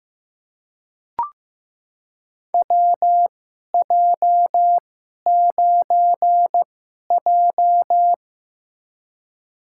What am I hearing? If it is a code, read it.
Morse code, heard as a keyed tone, about 15 wpm, spaced as for 11 wpm, 700 Hz. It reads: WJ9J
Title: USA Call Signs - 4 Characters - 2x Character Spacing - 15wpm